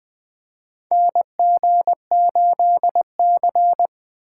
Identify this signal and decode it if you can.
Morse code, heard as a keyed tone, about 20 wpm, 700 Hz. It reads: NG8C
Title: USA Call Signs - 4 Characters - 8x Character Spacing - 20wpm